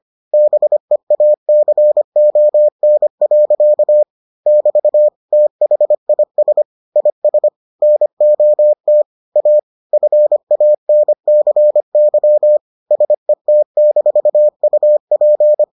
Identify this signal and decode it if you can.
Morse code, heard as a keyed tone, about 25 wpm, 600 Hz. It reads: BEACON. <BT> THIS IS NOT A FANCY SET-UP